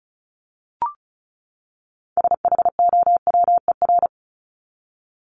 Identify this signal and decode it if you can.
Morse code, heard as a keyed tone, about 34 wpm, 700 Hz. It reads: SHOWER
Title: Top 1501-1750 Words 35wpm